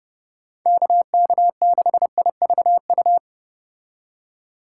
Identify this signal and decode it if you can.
Morse code, heard as a keyed tone, about 31 wpm, 700 Hz. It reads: KK6IVU